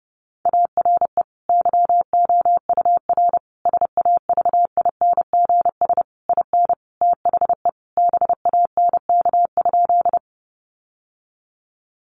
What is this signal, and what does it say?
Morse code, heard as a keyed tone, about 28 wpm, 700 Hz. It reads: ARE YOUR SAVINGS IN THE BANK?
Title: Sentences from Top 1200 Words 30wpm